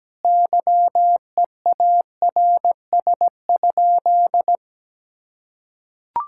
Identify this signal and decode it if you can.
Morse code, heard as a keyed tone, about 17 wpm, 700 Hz. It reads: YEARS?
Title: Sentences from Top 1000 Words - Review Part 6 - 17wpm